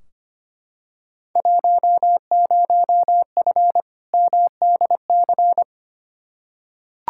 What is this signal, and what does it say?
Morse code, heard as a keyed tone, about 25 wpm, 700 Hz. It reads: 10F MDC
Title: Field Day Exchanges Encore - 25wpm